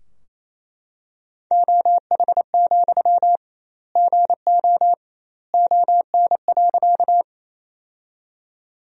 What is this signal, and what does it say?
Morse code, heard as a keyed tone, about 28 wpm, 700 Hz. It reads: OH, GO ON.
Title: 2x Word-Spacing - Sentences from Top 500 Words - 28wpm